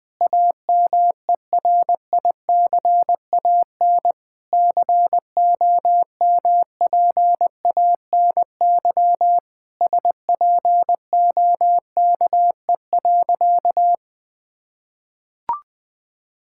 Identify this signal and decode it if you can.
Morse code, heard as a keyed tone, about 20 wpm, 700 Hz. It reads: AMERICAN COMPANY SPOKE.
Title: Sentences with 3-Word Phrases - 20wpm